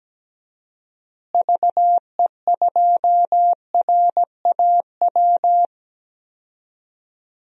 Morse code, heard as a keyed tone, about 17 wpm, 700 Hz. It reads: VE2RAW
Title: DXCC-100 Call Signs - 6 Characters - Repeated 3x - 17wpm